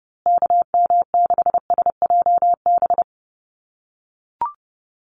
Morse code, heard as a keyed tone, about 30 wpm, 700 Hz. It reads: KM6SJB
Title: USA Call Signs Repeated 3x 30wpm